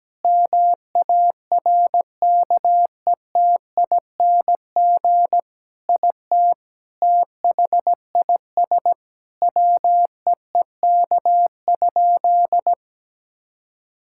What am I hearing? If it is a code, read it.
Morse code, heard as a keyed tone, about 17 wpm, 700 Hz. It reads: MARKETING IT THIS WEEK?
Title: Sentences from Top 1000 Words - Review Part 5 - 17wpm